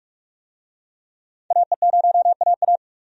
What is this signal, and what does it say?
Morse code, heard as a keyed tone, about 42 wpm, 700 Hz. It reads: AE0AA
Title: USA Call Signs - 5 Characters - 45wpm